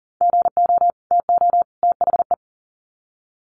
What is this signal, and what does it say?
Morse code, heard as a keyed tone, about 40 wpm, 700 Hz. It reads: GO TO THE
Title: Sets of 3 Words 40wpm